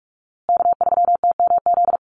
Morse code, heard as a keyed tone, about 44 wpm, 700 Hz. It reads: K3TMZ
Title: USA Call Signs Repeated 2x 45wpm